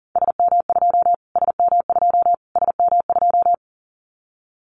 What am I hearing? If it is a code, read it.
Morse code, heard as a keyed tone, about 40 wpm, 700 Hz. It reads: SM2 SM2 SM2
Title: Callsign Prefixes for Top 25 DXCC - Repeated 3x - 40wpm